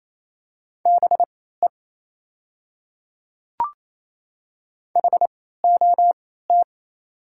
Morse code, heard as a keyed tone, about 28 wpm, spaced as for 16 wpm, 700 Hz. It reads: BE HOT
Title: 3x Character Spacing - Top 100 Words - 28wpm